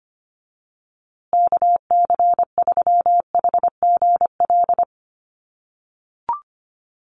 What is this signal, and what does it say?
Morse code, heard as a keyed tone, about 25 wpm, 700 Hz. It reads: KC3HGL
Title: USA Call Signs Repeated 2x 25wpm